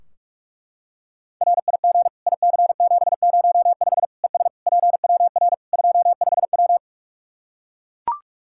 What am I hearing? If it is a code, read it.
Morse code, heard as a keyed tone, about 42 wpm, 700 Hz. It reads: RIG IC705 ES PWR 25W
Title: Practice QSO - Exchange 2 - Info - 45wpm